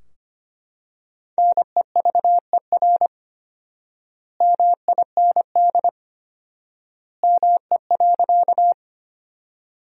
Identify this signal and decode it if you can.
Morse code, heard as a keyed tone, about 25 wpm, 700 Hz. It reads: NEVER MIND ME.
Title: 4x Word-Spacing - Sentences from Top 500 Words - 25wpm